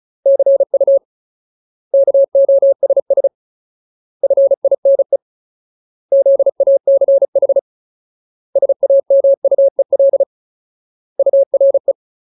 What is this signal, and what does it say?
Morse code, heard as a keyed tone, about 34 wpm, 550 Hz. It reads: CU KOSS FINE ZACH SAMUEL URE
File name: Ultimate Word List / 35WPM Ultimate